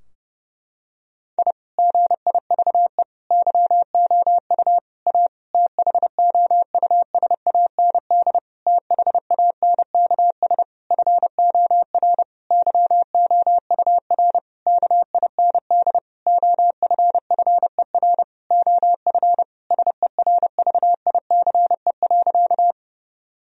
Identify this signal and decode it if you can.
Morse code, heard as a keyed tone, about 30 wpm, 700 Hz. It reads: I GIVE YOU A THOUSAND THANKS FOR YOUR KIND OFFER OF SERVICE.
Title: Sentences from Top 1000 Words - Review Part 4 - 30wpm